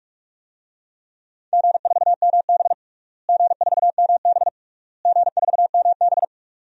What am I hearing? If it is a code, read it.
Morse code, heard as a keyed tone, about 47 wpm, 700 Hz. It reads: G4MB G4MB G4MB
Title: DXCC-100 Call Signs - 4 Characters - Repeated 3x - 45wpm